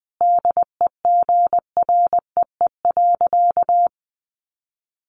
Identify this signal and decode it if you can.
Morse code, heard as a keyed tone, about 19 wpm, 700 Hz. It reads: DEGREE.
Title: Sentences from Top 1500 Words 20wpm